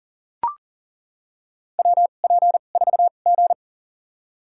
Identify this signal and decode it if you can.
Morse code, heard as a keyed tone, about 40 wpm, spaced as for 29 wpm, 700 Hz. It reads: WP4G
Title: USA Call Signs - 4 Characters - 2x Character Spacing - 40wpm